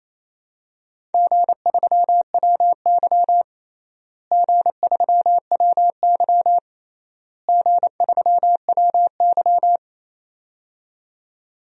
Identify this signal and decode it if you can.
Morse code, heard as a keyed tone, about 28 wpm, 700 Hz. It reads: G3WY G3WY G3WY
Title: DXCC-100 Call Signs - 4 Characters - Repeated 3x - 28wpm